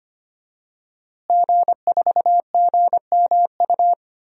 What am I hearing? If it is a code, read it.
Morse code, heard as a keyed tone, about 25 wpm, 700 Hz. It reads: G4GMU
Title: DXCC-100 Call Signs - 5 Characters - Repeated 3x - 25wpm